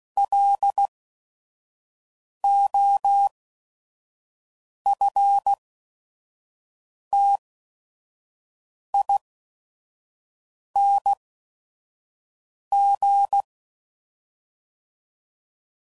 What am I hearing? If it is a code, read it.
Morse code, heard as a keyed tone, about 16 wpm, spaced as for 5 wpm, 800 Hz. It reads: LOFTING
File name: Prefixes / By First Letter / Ls